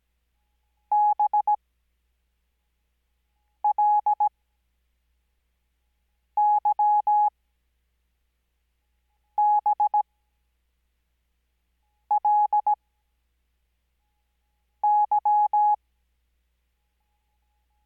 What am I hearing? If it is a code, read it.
Morse code, heard as a keyed tone, about 17 wpm, spaced as for 4 wpm, 840 Hz. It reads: BLYBLY
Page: Farnsworth Spacing: 15WPM Character speed Spaced at 5WPM.